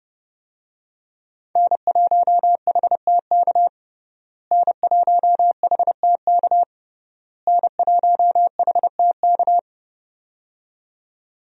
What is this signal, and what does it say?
Morse code, heard as a keyed tone, about 30 wpm, 700 Hz. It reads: N1HTK N1HTK N1HTK
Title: USA Call Signs - 5 Characters - Repeated 3x - 30wpm